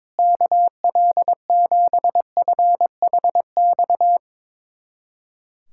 Morse code, heard as a keyed tone, about 22 wpm, 700 Hz. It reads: KL7FHX
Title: USA Callsign DE Callsign - 6 Characters - 22wpm